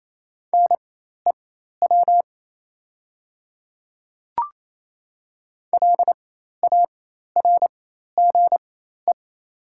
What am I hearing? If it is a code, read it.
Morse code, heard as a keyed tone, about 27 wpm, spaced as for 13 wpm, 700 Hz. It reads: NEW LARGE
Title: 4x Character Spacing - Top 200 Words - 28wpm